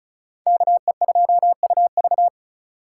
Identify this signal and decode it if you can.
Morse code, heard as a keyed tone, about 36 wpm, 700 Hz. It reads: KE2UV
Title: USA Call Signs - 5 Characters - 4x Character Spacing - 35wpm